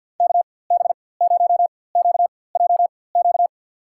Morse code, heard as a keyed tone, about 47 wpm, spaced as for 23 wpm, 700 Hz. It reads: KB0QJQ
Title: USA Call Signs - 6 Characters - 4x Character Spacing - 50wpm